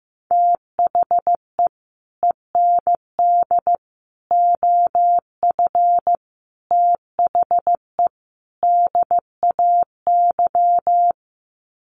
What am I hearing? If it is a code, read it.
Morse code, heard as a keyed tone, about 15 wpm, 700 Hz. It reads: THE END OF THE DAY
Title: Sets of 5 Words 15wpm